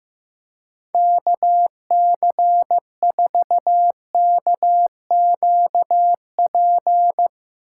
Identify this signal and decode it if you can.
Morse code, heard as a keyed tone, about 15 wpm, 700 Hz. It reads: KC4KQP
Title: Speed Racing - USA Call Signs 15wpm